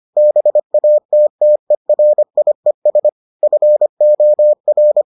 Morse code, heard as a keyed tone, about 25 wpm, 600 Hz. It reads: BATTERIES FOR